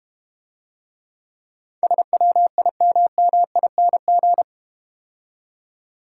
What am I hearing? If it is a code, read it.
Morse code, heard as a keyed tone, about 31 wpm, 690 Hz. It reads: SWIMMING
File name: practice / 25wpm / Difficult Words - Speed-Racing with Hints - 25wpm